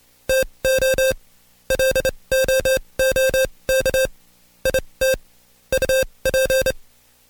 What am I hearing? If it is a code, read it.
Morse code, heard as a keyed tone, about 27 wpm, spaced as for 22 wpm, 530 Hz. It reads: TO LOOK IT UP
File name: Code Quick / AudioDownld2 / Seg137-Stepper10 21WPM